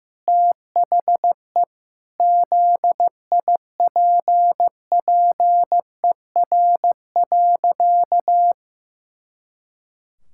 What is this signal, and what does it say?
Morse code, heard as a keyed tone, about 16 wpm, 700 Hz. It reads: THE ZIPPER.
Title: Pangram Challenge v2 - 15wpm